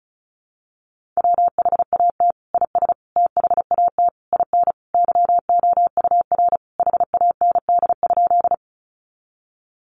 Morse code, heard as a keyed tone, about 34 wpm, 700 Hz. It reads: WHAT IS THAT IN YOUR HAND?
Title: Sentences from Top 200 Words 35wpm